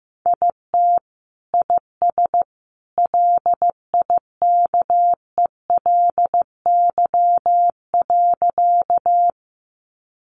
Morse code, heard as a keyed tone, about 15 wpm, 700 Hz. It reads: IT IS LIKELY.
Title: Sentences from Top 600 Words 15wpm